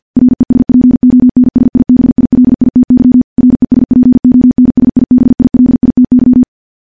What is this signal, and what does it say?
Morse code, heard as a keyed tone, about 50 wpm, 250 Hz. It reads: RESPONSIBILITY RESPONSIBILITY
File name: morse / morse-training-50WPM